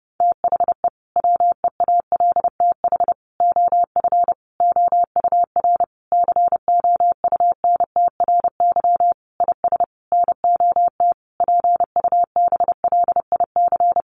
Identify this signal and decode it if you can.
Morse code, heard as a keyed tone, about 30 wpm, 700 Hz. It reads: THE WEALTH OF OUR COUNTRY IS NOT PUBLIC
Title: Sentences from Top 1200 Words 30wpm